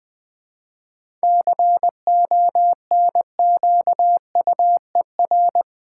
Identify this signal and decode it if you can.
Morse code, heard as a keyed tone, about 20 wpm, 700 Hz. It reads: CONQUER
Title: 7 Letter Words - First 3 Letters Given - 20wpm